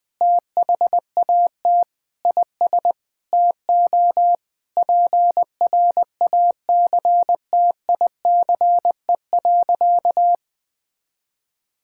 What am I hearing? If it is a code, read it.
Morse code, heard as a keyed tone, about 21 wpm, 700 Hz. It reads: THAT IS TO PRACTICE.
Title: Sentences with 9-Word Phrases - 20wpm